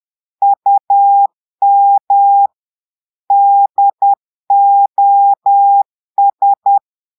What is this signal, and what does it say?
Morse code, heard as a keyed tone, about 10 wpm, 800 Hz. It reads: UM DOS
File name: Portugais - texte en clair 1 - vitesse - 10 mots minute